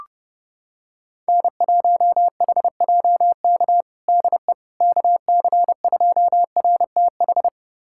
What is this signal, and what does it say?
Morse code, heard as a keyed tone, about 32 wpm, 700 Hz. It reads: N1HJK DE KC2RTH
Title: USA Callsign DE Callsign 30wpm